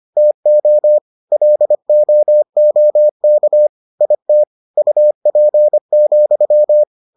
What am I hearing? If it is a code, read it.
Morse code, heard as a keyed tone, about 25 wpm, 600 Hz. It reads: TO LOOK IT UP,